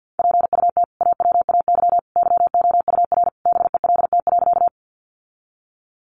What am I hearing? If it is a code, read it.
Morse code, heard as a keyed tone, about 50 wpm, 700 Hz. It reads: PUT AWAY YOUR BELT.
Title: Sentences from Top 1400 Words 50wpm